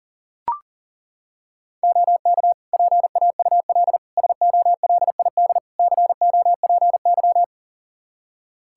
Morse code, heard as a keyed tone, about 40 wpm, 700 Hz. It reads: OK PAUL SOLID COPY
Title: K1USN SST QSO Practice 40wpm